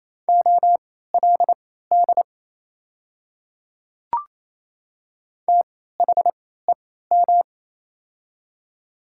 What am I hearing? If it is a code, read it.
Morse code, heard as a keyed tone, about 28 wpm, spaced as for 16 wpm, 700 Hz. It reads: OLD THEM
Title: 3x Character Spacing - Top 200 Words - 28wpm